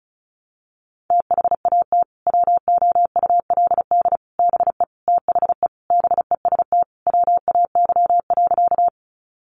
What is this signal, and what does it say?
Morse code, heard as a keyed tone, about 34 wpm, 700 Hz. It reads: THAT WOULD BE THE BEST WAY.
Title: Sentences from Top 400 Words 35wpm